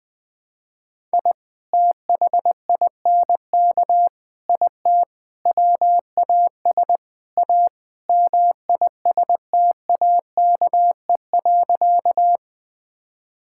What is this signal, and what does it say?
Morse code, heard as a keyed tone, about 20 wpm, 700 Hz. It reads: I THINK IT WAS A MISTAKE.
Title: Sentences with 5-Word Phrases - 20wpm